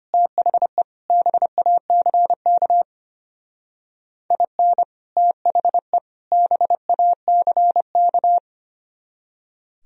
Morse code, heard as a keyed tone, about 31 wpm, 700 Hz. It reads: THE BACK IN THE BACK
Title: Speed-Racing - Sets of 3 Words 25wpm